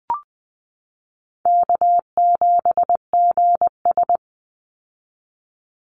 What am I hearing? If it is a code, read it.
Morse code, heard as a keyed tone, about 19 wpm, 700 Hz. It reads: K7GS
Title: Easy Call Signs Repeated 3x - 20wpm